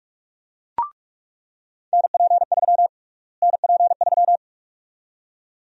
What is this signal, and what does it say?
Morse code, heard as a keyed tone, about 46 wpm, 700 Hz. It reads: NP3 NP3
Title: USA Call Sign Prefixes - Repeated 2x - 45wpm